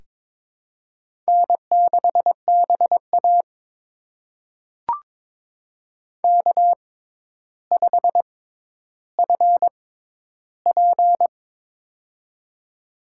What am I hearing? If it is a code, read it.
Morse code, heard as a keyed tone, about 22 wpm, spaced as for 11 wpm, 700 Hz. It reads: N6BA K5FP
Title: USA Call Signs - 4 Characters - 6x Character Spacing - 22wpm